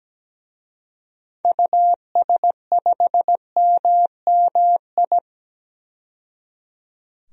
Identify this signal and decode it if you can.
Morse code, heard as a keyed tone, about 17 wpm, 700 Hz. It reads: US5MMI
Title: DXCC-100 Call Signs - 6 Characters - 17wpm